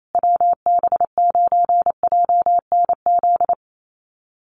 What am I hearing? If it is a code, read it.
Morse code, heard as a keyed tone, about 28 wpm, 700 Hz. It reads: WB9JNZ
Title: USA Call Signs Repeated 2x 28wpm